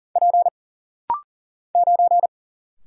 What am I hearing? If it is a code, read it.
Morse code, heard as a keyed tone, about 40 wpm, 700 Hz. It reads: P 9
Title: Single Letter-Number Alt - Rapid-Fire 40wpm